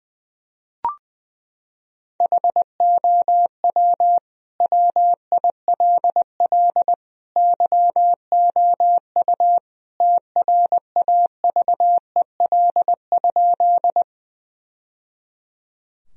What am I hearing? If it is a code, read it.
Morse code, heard as a keyed tone, about 20 wpm, 700 Hz. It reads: HOW WILL YOU TRAVEL?
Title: Sentences from Top 1000 Words - Review Part 2 - 20wpm